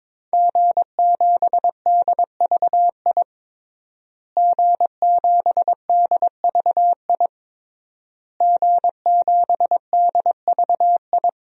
Morse code, heard as a keyed tone, about 22 wpm, 700 Hz. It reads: G7DVI G7DVI G7DVI
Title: DXCC-100 Call Signs - 5 Characters - Repeated 3x - 22wpm